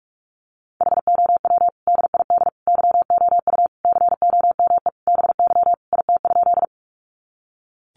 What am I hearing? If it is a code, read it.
Morse code, heard as a keyed tone, about 44 wpm, 700 Hz. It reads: HOW DID YOU COME BY IT?
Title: Sentences from Top 100 Words 45wpm